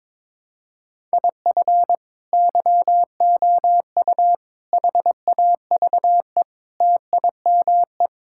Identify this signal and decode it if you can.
Morse code, heard as a keyed tone, about 22 wpm, 700 Hz. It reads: IF YOU HAVE TIME
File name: practice / 22wpm / Sentences with 3-Word Phrases - 22wpm